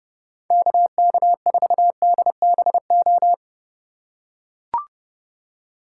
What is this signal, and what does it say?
Morse code, heard as a keyed tone, about 30 wpm, 700 Hz. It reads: KK4DBO